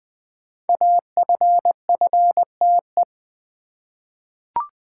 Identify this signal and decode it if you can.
Morse code, heard as a keyed tone, about 20 wpm, 700 Hz. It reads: AFFTE
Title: Code Groups - 5 Letters - 20wpm